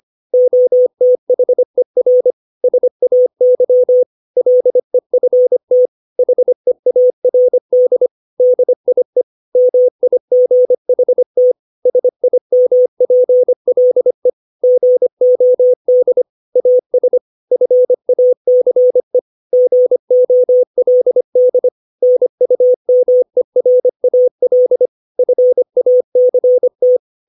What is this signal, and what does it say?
Morse code, heard as a keyed tone, about 25 wpm, 500 Hz. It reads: OTHER SAY LEFT HEARD DIE MIGHT SIMPLE GOD AS FACE GOLD NUMERAL FACT
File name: 632 words / 632 Words 25WPM